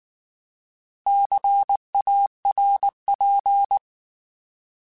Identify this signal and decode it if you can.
Morse code, heard as a keyed tone, about 19 wpm, 800 Hz. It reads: CARP
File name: Alphabetical / Alphabetical mp3s / Cs @18